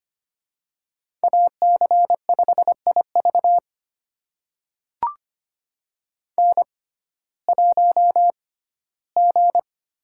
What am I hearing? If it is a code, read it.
Morse code, heard as a keyed tone, about 25 wpm, spaced as for 15 wpm, 700 Hz. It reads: AC5IV N1G
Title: USA Call Signs - 5 Characters - 6x Character Spacing - 25wpm